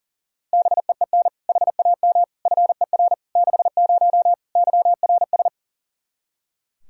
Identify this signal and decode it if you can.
Morse code, heard as a keyed tone, about 40 wpm, 700 Hz. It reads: BEEN HAM FER 60 YRS
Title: Practice QSO - Exchange 3 - More Info - 40wpm